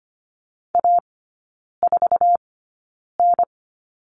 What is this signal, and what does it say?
Morse code, heard as a keyed tone, about 25 wpm, spaced as for 9 wpm, 690 Hz. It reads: A4N
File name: Lesson 15e - Number 5 - Three Characters 25wpm